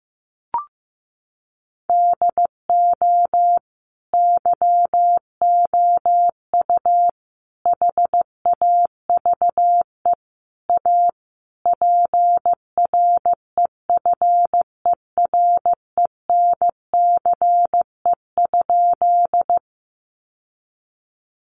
Morse code, heard as a keyed tone, about 15 wpm, 700 Hz. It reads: DO YOU HAVE A PREFERENCE?
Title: Sentences from Top 1500 Words 15wpm